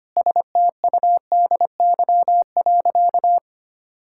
Morse code, heard as a keyed tone, about 25 wpm, 700 Hz. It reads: STUDY.
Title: Sentences from Top 1000 Words - Review Part 5 - 25wpm